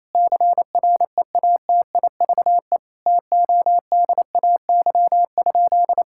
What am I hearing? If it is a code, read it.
Morse code, heard as a keyed tone, about 27 wpm, 700 Hz. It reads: CREATIVE TODAY?
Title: Sentences from Top 1000 Words - Review Part 5 - 28wpm